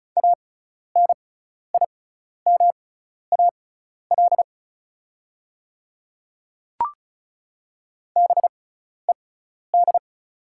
Morse code, heard as a keyed tone, about 34 wpm, spaced as for 12 wpm, 700 Hz. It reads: ANIMAL BED
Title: 6x Character Spacing - Top 500 Words - 35wpm